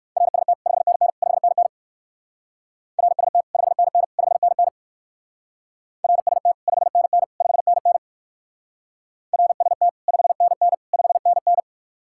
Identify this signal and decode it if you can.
Morse code, heard as a keyed tone, about 55 wpm, 700 Hz. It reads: RST 5NN 5NN RST 5NN 5NN RST 5NN 5NN RST 5NN 5NN
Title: Speed-Racing Common QSO Phrases Encore - 45wpm